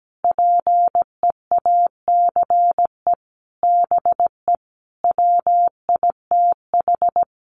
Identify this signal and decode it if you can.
Morse code, heard as a keyed tone, about 17 wpm, 700 Hz. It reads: PEACE BE WITH
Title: Sentences from Top 1500 Words 17wpm